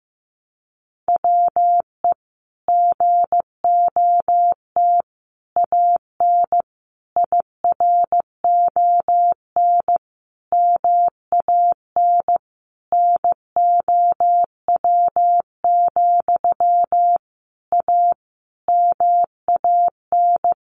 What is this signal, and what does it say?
Morse code, heard as a keyed tone, about 15 wpm, 700 Hz. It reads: WE GOT AN IRON MAN NOW, A MAN
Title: A Sound Of Thunder 15wpm